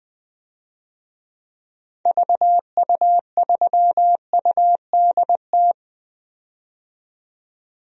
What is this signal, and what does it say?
Morse code, heard as a keyed tone, about 20 wpm, 700 Hz. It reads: VU3UDT
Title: DXCC-100 Call Signs - 6 Characters - 20wpm